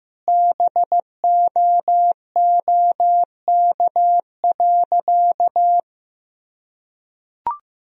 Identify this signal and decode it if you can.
Morse code, heard as a keyed tone, about 15 wpm, 700 Hz. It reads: BOOK.
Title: Sentences from Top 1000 Words - Review Part 2 - 15wpm